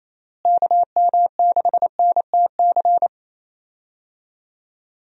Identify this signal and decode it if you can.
Morse code, heard as a keyed tone, about 28 wpm, 700 Hz. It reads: KM6NTC
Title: USA Call Signs - 6 Characters - Repeated 3x - 28wpm